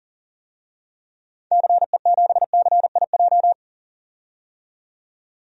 Morse code, heard as a keyed tone, about 39 wpm, 700 Hz. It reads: CE7CIJ
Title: DXCC-100 Call Signs - 6 Characters - 40wpm